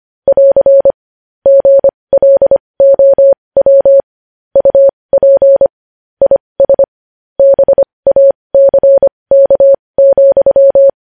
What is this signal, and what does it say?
Morse code, heard as a keyed tone, about 25 wpm, spaced as for 20 wpm, 550 Hz. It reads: <AR> GLOW UP IS BACK,
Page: news headlines in morse code at 20 wpm